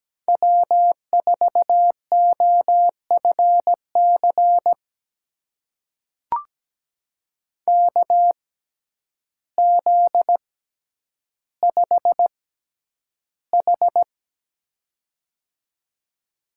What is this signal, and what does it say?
Morse code, heard as a keyed tone, about 17 wpm, spaced as for 9 wpm, 700 Hz. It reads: W4OFC KZ5H
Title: USA Call Signs - 6x Character Spacing - 17wpm